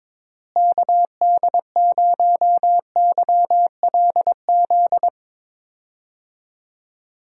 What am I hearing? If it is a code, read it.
Morse code, heard as a keyed tone, about 21 wpm, 700 Hz. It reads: KD0YLZ
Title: USA Call Signs - 6 Characters - Repeated 2x - 22wpm